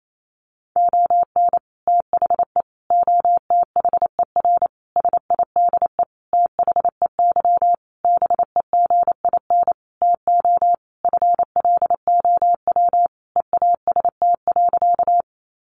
Morse code, heard as a keyed tone, about 28 wpm, 700 Hz. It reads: ON THE OTHER SIDE THEY BEGIN TO FLOW EAST.
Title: Sentences from Top 1200 Words 28wpm